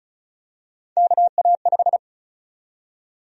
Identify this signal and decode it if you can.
Morse code, heard as a keyed tone, about 36 wpm, 700 Hz. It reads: KA5
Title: USA Call Sign Prefixes - 35wpm